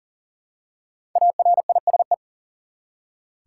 Morse code, heard as a keyed tone, about 40 wpm, 700 Hz. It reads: ARISE
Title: Five Letter Words - 40wpm